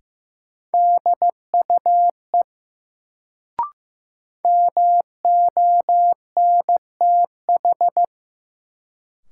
Morse code, heard as a keyed tone, about 15 wpm, 700 Hz. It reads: DUE MONTH
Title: Sets of 5 Words Encore - New Word Drill - 15wpm